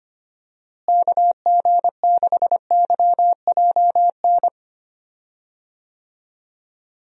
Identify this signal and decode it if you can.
Morse code, heard as a keyed tone, about 25 wpm, 700 Hz. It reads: KG6YJN
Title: USA Call Signs - 6 Characters - Repeated 2x - 25wpm